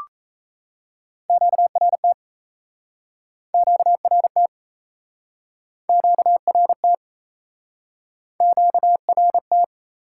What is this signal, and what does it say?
Morse code, heard as a keyed tone, about 40 wpm, 700 Hz. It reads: QRT QRT QRT QRT